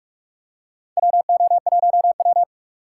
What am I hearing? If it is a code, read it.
Morse code, heard as a keyed tone, about 46 wpm, 700 Hz. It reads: WO1W
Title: USA Call Signs - 4 Characters - 6x Character Spacing - 45wpm